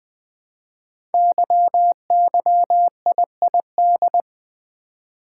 Morse code, heard as a keyed tone, about 20 wpm, 700 Hz. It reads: YYIID
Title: Code Groups - 5 Letter:Numbers - 20wpm